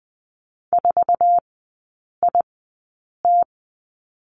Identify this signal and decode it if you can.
Morse code, heard as a keyed tone, about 20 wpm, spaced as for 8 wpm, 700 Hz. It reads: 4IT
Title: 047 Lesson 9e - Number 4 - Three Characters 20wpm